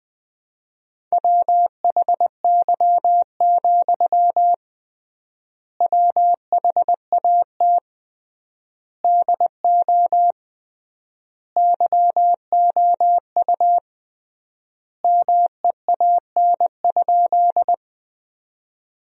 Morse code, heard as a keyed tone, about 20 wpm, 700 Hz. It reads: WHY, WHAT DO YOU MEAN?
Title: 3x Word-Spacing - Sentences from Top 200 Words - 20wpm